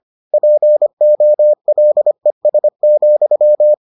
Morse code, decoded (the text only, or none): POLES,